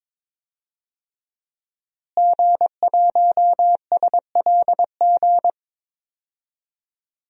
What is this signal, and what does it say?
Morse code, heard as a keyed tone, about 22 wpm, 700 Hz. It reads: G1SLG
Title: DXCC-100 Call Signs - 5 Characters - Repeated 2x - 22wpm